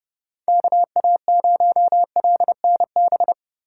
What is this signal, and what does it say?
Morse code, heard as a keyed tone, about 29 wpm, 700 Hz. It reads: KA0LNB